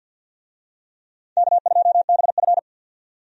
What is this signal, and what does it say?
Morse code, heard as a keyed tone, about 51 wpm, 700 Hz. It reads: K2BF
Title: USA Call Signs - 4 Characters - 50wpm